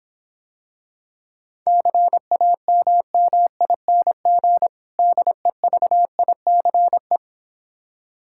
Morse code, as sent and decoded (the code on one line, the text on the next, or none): -.-. .- -- -- .. -. --. / -.. ...- .. -.-. .
CAMMING DEVICE